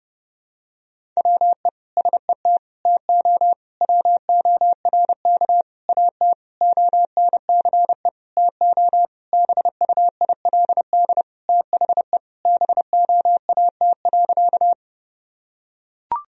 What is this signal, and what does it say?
Morse code, heard as a keyed tone, about 32 wpm, 700 Hz. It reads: WE SET TO WORK AT ONCE TO BUILD THE BOAT.
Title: Sentences from Top 1000 Words - Review Part 1 - 30wpm